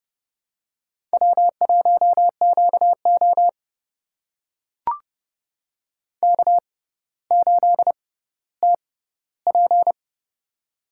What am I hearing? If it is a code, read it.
Morse code, heard as a keyed tone, about 30 wpm, spaced as for 15 wpm, 700 Hz. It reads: W1QO K8TP
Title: USA Call Signs - 4 Characters - 6x Character Spacing - 30wpm